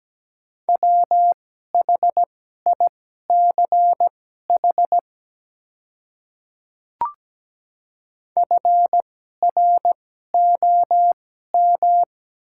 Morse code, heard as a keyed tone, about 17 wpm, spaced as for 12 wpm, 700 Hz. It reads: WHICH FROM
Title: 2x Character Spacing - Top 100 Words - 17wpm